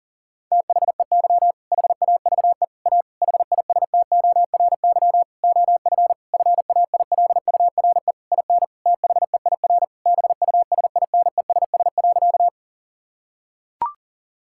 Morse code, heard as a keyed tone, about 40 wpm, 700 Hz. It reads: THEY HAVE A HISTORY OF FAILURE IN THEIR BUSINESS.
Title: Sentences from Top 1000 Words - Review Part 6 - 40wpm